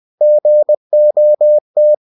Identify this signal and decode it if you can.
Morse code, heard as a keyed tone, about 20 wpm, 600 Hz. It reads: GOT
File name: cw-books / winnie-the-pooh / winnie-20wpm-page-0083